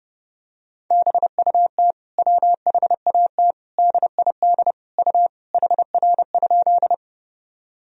Morse code, heard as a keyed tone, about 32 wpm, 700 Hz. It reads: BUT WHAT DID U HR?
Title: Sentences from Top 300 Words and Abbreviations 30wpm